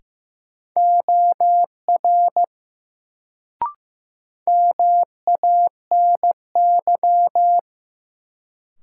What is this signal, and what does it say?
Morse code, heard as a keyed tone, about 15 wpm, 700 Hz. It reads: OR MANY
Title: Sets of 2 Words - New Word Drill - 15wpm